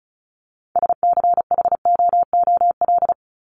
Morse code, heard as a keyed tone, about 34 wpm, 700 Hz. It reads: SCHOOL
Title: Top 300 Words 35wpm